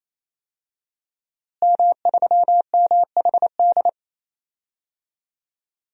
Morse code, heard as a keyed tone, about 28 wpm, 700 Hz. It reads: M3MHD
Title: DXCC-100 Call Signs - 5 Characters - Repeated 3x - 28wpm